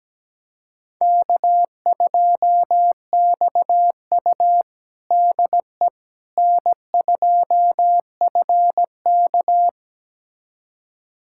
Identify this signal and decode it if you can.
Morse code, heard as a keyed tone, about 17 wpm, 700 Hz. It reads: K2XU DE N2FK
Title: USA Callsign DE Callsign - 4 Characters - 17wpm